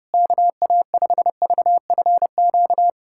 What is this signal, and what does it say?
Morse code, heard as a keyed tone, about 30 wpm, 700 Hz. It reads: KA5VFQ
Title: USA Callsign DE Callsign - 6 Characters - 30wpm